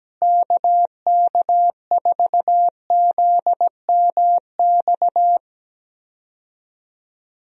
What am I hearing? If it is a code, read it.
Morse code, heard as a keyed tone, about 17 wpm, 700 Hz. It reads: KK4ZMX
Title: USA Call Signs - 6 Characters - Repeated 2x - 17wpm